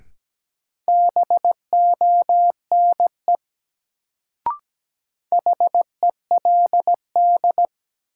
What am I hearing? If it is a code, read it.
Morse code, heard as a keyed tone, about 17 wpm, 700 Hz. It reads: BONE HELD
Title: Sentences from Top 1600 Words - New Word Drill - 17wpm